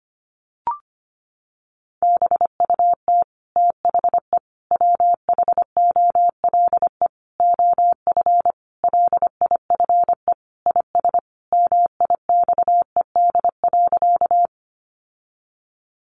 Morse code, heard as a keyed tone, about 25 wpm, 700 Hz. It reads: BUT THE WHOLE OF LIFE IS MIXED.
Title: Sentences from Top 1500 Words 25wpm